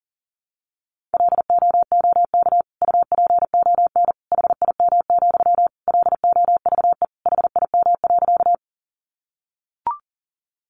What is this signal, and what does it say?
Morse code, heard as a keyed tone, about 40 wpm, 700 Hz. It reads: LOOK UPON HIM, LOVE HIM.
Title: Sentences from Top 1100 Words 40wpm